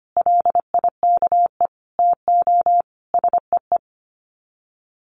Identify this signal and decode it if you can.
Morse code, heard as a keyed tone, about 25 wpm, 700 Hz. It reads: LIKE TO SEE
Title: Sets of 5 Words Encore2 25wpm